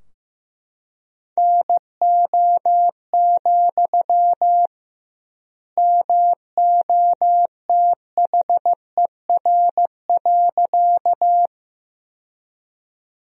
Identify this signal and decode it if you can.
Morse code, heard as a keyed tone, about 15 wpm, 700 Hz. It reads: NO, MOTHER.
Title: 2x Word-Spacing - Sentences from Top 200 Words - 15wpm